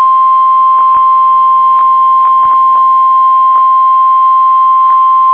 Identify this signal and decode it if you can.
Emergency Alert System (SAME) tones (the 1050 Hz attention tone). It is heard from 0.2 to 4.6 s.